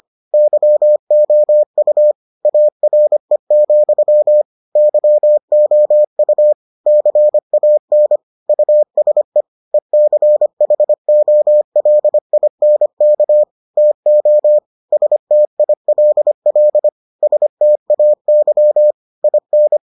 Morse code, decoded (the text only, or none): YOU ARE, YOU CAN USE ECHOLINK TO STILL STAY IN